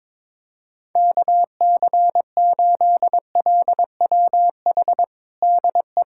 Morse code, heard as a keyed tone, about 22 wpm, 700 Hz. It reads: KC8LWH DE
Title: USA Callsign DE Callsign 22wpm